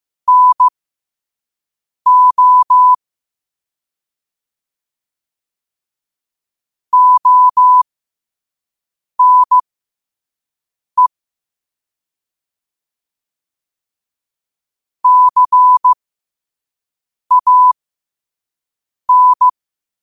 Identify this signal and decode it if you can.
Morse code, heard as a keyed tone, about 15 wpm, spaced as for 5 wpm, 1000 Hz. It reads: NO ONE CAN
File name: 5wpm